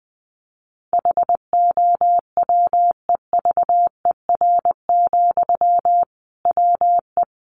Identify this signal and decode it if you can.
Morse code, heard as a keyed tone, about 20 wpm, 700 Hz. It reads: HOWEVER, WE
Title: Sentences from Top 600 Words 20wpm